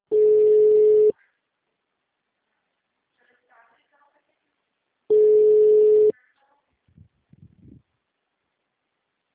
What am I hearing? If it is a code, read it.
Ringback tone (Europe, 425 Hz), with speech.
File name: out- -2055- - - .28648